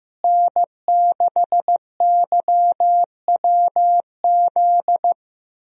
Morse code, heard as a keyed tone, about 16 wpm, 700 Hz. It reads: N6YWZ
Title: USA Call Signs - 5 Characters - Repeated 2x - 15wpm